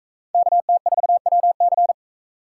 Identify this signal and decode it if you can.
Morse code, heard as a keyed tone, about 44 wpm, 700 Hz. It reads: KT4WC